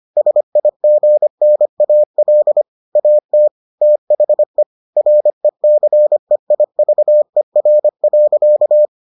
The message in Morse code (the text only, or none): SIGNAL AT THE RECEIVER.